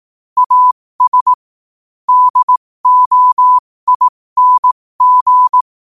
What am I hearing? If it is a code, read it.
Morse code, heard as a keyed tone, about 18 wpm, spaced as for 15 wpm, 1000 Hz. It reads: AS DOING